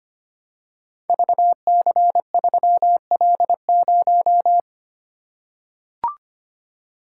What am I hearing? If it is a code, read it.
Morse code, heard as a keyed tone, about 25 wpm, 700 Hz. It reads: VC3L0
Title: Code Groups - 5 Letter:Numbers - 25wpm